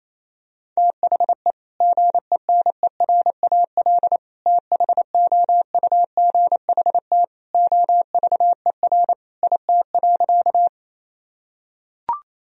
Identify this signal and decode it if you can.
Morse code, heard as a keyed tone, about 28 wpm, 700 Hz. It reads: THE GENERAL THOUGHT OVER IT.
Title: Sentences from Top 1000 Words - Review Part 3 - 28wpm